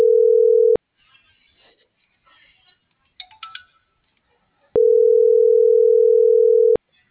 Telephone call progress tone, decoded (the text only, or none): ringback tone